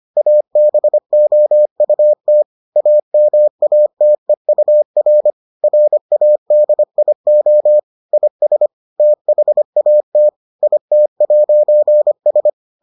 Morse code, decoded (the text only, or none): ABOUT AMATEUR RADIO IS THAT IT'S